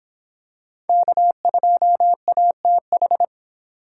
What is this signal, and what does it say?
Morse code, heard as a keyed tone, about 25 wpm, 700 Hz. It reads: K2ATH